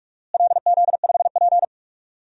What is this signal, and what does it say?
Morse code, heard as a keyed tone, about 46 wpm, 700 Hz. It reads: LZ5P
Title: DXCC-100 Call Signs - 4 Characters - Repeated 2x - 45wpm